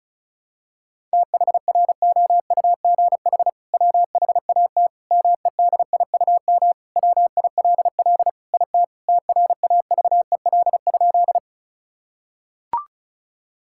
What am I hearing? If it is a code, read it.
Morse code, heard as a keyed tone, about 34 wpm, 700 Hz. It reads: THROUGH WHAT MEDIUM WILL IT TRAVEL?
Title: Sentences from Top 1000 Words - Review Part 6 - 35wpm